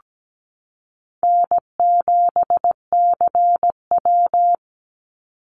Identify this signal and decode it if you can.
Morse code, heard as a keyed tone, about 17 wpm, 700 Hz. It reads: N7CW
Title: NA SOTA Chaser Callsigns 17wpm